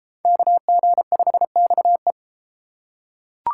Morse code, heard as a keyed tone, about 30 wpm, 700 Hz. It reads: KG5XE